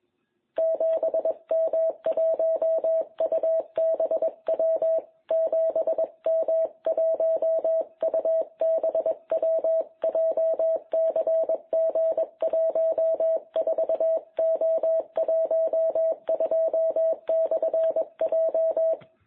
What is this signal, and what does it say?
Morse code, heard as a keyed tone, about 21 wpm, 660 Hz. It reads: 7M1UBW 7M1UBW JCG14O12/J
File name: jh8cbh / 2021cw